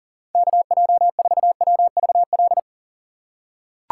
Morse code, heard as a keyed tone, about 39 wpm, 700 Hz. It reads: KJ4WVL
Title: USA Call Signs - 6 Characters - 2x Character Spacing - 40wpm